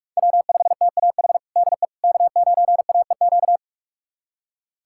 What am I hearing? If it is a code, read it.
Morse code, heard as a keyed tone, about 46 wpm, 700 Hz. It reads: W5TAH DE K9AEQ
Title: USA Callsign DE Callsign - 5 Characters - 45wpm